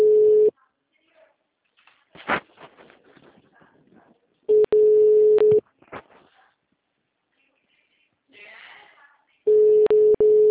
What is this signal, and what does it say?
Ringback tone (Europe, 425 Hz), with speech.
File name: out- -2017- - - .16665